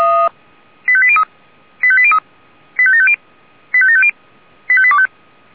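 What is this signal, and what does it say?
Five-tone selective call (ZVEI), heard as quick runs of five tones, tones of about 70 ms. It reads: 85803 85803 85750 85750 85825
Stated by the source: ZVEI five-tone call 85827 received on SDR.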